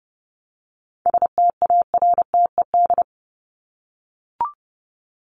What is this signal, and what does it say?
Morse code, heard as a keyed tone, about 30 wpm, 700 Hz. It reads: STARTED
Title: ED Suffix - Top 100 Words with Base Words 30wpm